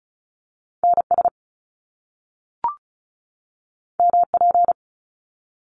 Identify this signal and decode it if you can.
Morse code, heard as a keyed tone, about 35 wpm, 700 Hz. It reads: N S M P
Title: Top Two Letter Combinations in Words 35wpm